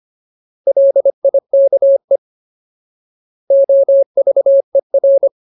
Morse code, heard as a keyed tone, about 25 wpm, 550 Hz. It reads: LIKE OVER
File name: Ultimate Word List / 25WPM Ultimate